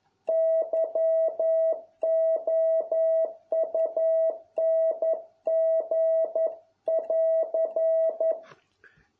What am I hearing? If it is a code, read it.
Morse code, heard as a keyed tone, about 11 wpm, 650 Hz. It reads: YOUNG<AR>